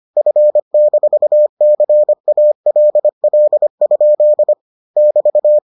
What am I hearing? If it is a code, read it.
Morse code, heard as a keyed tone, about 25 wpm, 600 Hz. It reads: F-CALL? <BT>